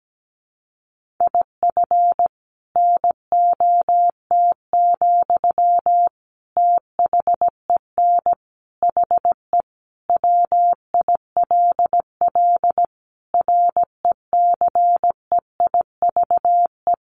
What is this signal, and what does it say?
Morse code, heard as a keyed tone, about 17 wpm, 700 Hz. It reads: IF NOT, THEN HE WILL RECEIVE
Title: Sentences from Top 1500 Words 17wpm